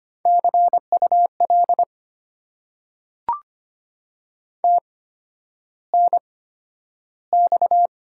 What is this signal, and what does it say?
Morse code, heard as a keyed tone, about 25 wpm, spaced as for 11 wpm, 700 Hz. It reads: CUL TNX
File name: Top 90 QSO Elements - 8x Character Spacing - 25wpm